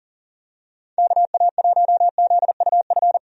Morse code, heard as a keyed tone, about 39 wpm, 700 Hz. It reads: KA1ZUF
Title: USA Call Signs - 6 Characters - 40wpm